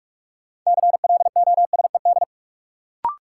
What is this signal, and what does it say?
Morse code, heard as a keyed tone, about 46 wpm, 700 Hz. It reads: CLOSED
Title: Sentences from Top 1400 Words - New Word Drill Alt - 45wpm